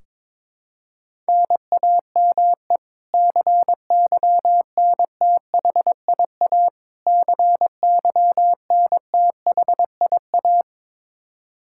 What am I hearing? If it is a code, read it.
Morse code, heard as a keyed tone, about 22 wpm, 700 Hz. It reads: NAME CYNTHIA CYNTHIA
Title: K1USN SST QSO Practice 22wpm